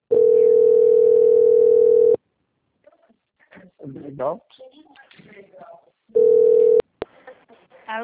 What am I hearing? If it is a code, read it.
Ringback tone (North America), 440 + 480 Hz, with speech.
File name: out- - - - - .18544